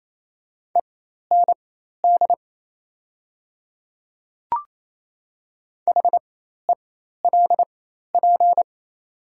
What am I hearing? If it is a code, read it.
Morse code, heard as a keyed tone, about 27 wpm, spaced as for 13 wpm, 690 Hz. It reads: END HELP
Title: 4x Character Spacing - Top 200 Words - 28wpm